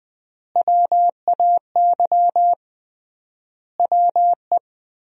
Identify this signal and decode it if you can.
Morse code, heard as a keyed tone, about 21 wpm, 700 Hz. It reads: WAY WE
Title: 3x Word-Spacing - Sentences from Top 200 Words - 20wpm